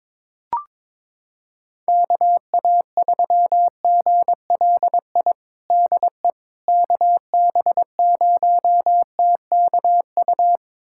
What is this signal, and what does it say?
Morse code, heard as a keyed tone, about 22 wpm, 700 Hz. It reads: KA3GLI DE KB0TKU
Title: USA Callsign DE Callsign - 6 Characters - 22wpm